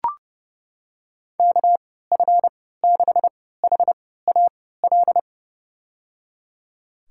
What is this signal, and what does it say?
Morse code, heard as a keyed tone, about 30 wpm, spaced as for 17 wpm, 700 Hz. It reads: KF6HAL
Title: USA Call Signs - 6 Characters - 3x Character Spacing - 30wpm